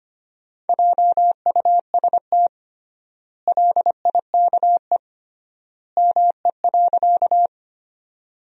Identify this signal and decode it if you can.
Morse code, heard as a keyed tone, about 25 wpm, 700 Hz. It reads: JUST LIKE ME.
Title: 3x Word-Spacing - Sentences from Top 200 Words - 25wpm